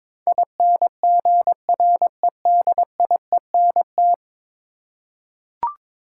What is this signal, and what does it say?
Morse code, heard as a keyed tone, about 22 wpm, 700 Hz. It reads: INGREDIENT
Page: morse-code-ninja-practice-22wpm